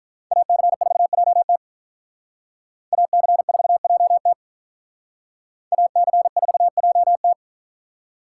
Morse code, heard as a keyed tone, about 47 wpm, 700 Hz. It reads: AC4JT AC4JT AC4JT